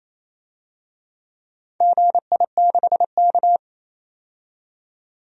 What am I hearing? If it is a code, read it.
Morse code, heard as a keyed tone, about 28 wpm, 700 Hz. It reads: GI6K
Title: DXCC-100 Call Signs - 4 Characters - 28wpm